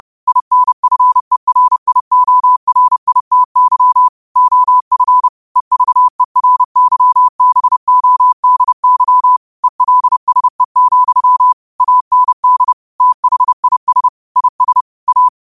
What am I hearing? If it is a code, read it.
Morse code, heard as a keyed tone, about 30 wpm, 1000 Hz. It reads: INFERIORITY OF EVERYBODY ELSE, AND THIS IS A
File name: qotdfiles / 30wpm